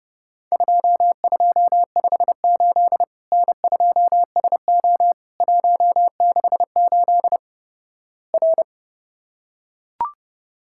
Morse code, heard as a keyed tone, about 30 wpm, 700 Hz. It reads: 2258 N2SO 168 R
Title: Worked All Europe DX Contest - QTC Exchange - 30wpm